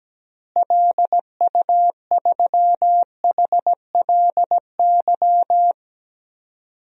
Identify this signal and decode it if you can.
Morse code, heard as a keyed tone, about 17 wpm, 700 Hz. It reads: LU3HLY
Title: DXCC-100 Call Signs - 6 Characters - Repeated 3x - 17wpm